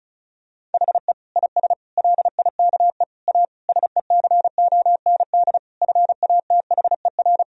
Morse code, heard as a keyed tone, about 36 wpm, 700 Hz. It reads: HE IS LIKE A SECOND FATHER
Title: Sentences from Top 1000 Words - Review Part 1 - 35wpm